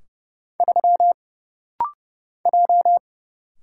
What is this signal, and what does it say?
Morse code, heard as a keyed tone, about 30 wpm, 700 Hz. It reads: 3 J
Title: Single Letter-Number Alt - Rapid-Fire 30wpm